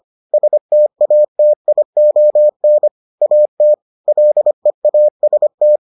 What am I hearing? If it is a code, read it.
Morse code, heard as a keyed tone, about 25 wpm, 600 Hz. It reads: STATION AT LEAST